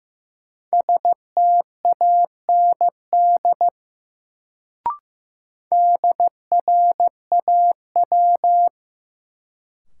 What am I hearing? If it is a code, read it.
Morse code, heard as a keyed tone, about 15 wpm, 700 Hz. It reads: STAND DRAW
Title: Sentences from Top 300 Words - New Word Drill - 15wpm